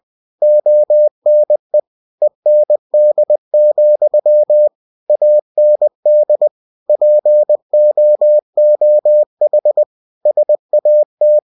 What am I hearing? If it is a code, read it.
Morse code, heard as a keyed tone, about 20 wpm, 600 Hz. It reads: ONE END, AND POOH SAT
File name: cw-books / winnie-the-pooh / winnie-20wpm-page-0106